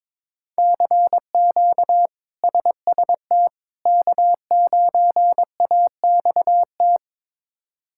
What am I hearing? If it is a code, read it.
Morse code, heard as a keyed tone, about 22 wpm, 700 Hz. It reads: CQ SST K9AXT
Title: K1USN SST QSO Practice - 22wpm